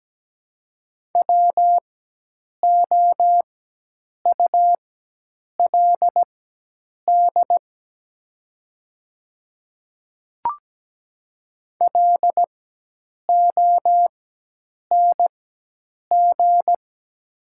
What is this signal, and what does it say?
Morse code, heard as a keyed tone, about 17 wpm, spaced as for 8 wpm, 700 Hz. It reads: WOULD LONG
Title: 4x Character Spacing - Top 100 Words - 17wpm